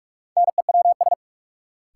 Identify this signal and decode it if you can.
Morse code, heard as a keyed tone, about 46 wpm, 700 Hz. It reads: NEWS
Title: Sentences from Top 800 Words - New Word Drill - 45wpm